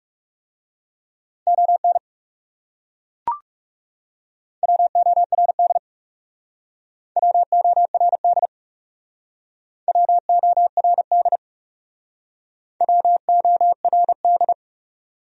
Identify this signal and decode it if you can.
Morse code, heard as a keyed tone, about 37 wpm, 700 Hz. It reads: ON WORD WORD WORD WORD
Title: Speed-Racing Top 200 Words 30wpm